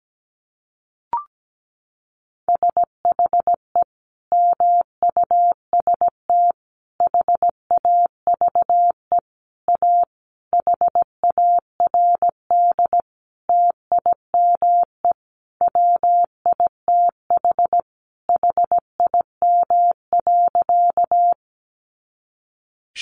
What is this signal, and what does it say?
Morse code, heard as a keyed tone, about 17 wpm, 700 Hz. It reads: SHE MUST HAVE A HARD TIME WITH HIM.
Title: Sentences from Top 300 Words 17wpm